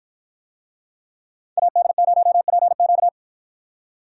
Morse code, heard as a keyed tone, about 53 wpm, 700 Hz. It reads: AD0PQ